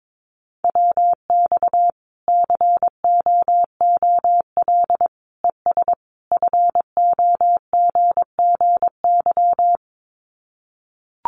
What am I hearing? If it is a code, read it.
Morse code, heard as a keyed tone, about 22 wpm, 700 Hz. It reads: WX COOL ES FOGGY
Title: Common QSO Phrases 22wpm